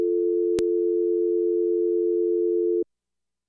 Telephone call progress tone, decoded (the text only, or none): dial tone